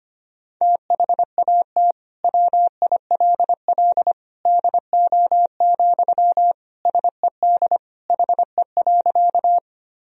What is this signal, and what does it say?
Morse code, heard as a keyed tone, about 25 wpm, 700 Hz. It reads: THAT WILL DO, SED HE.
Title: Sentences from Top 100 Words and Abbreviations 25wpm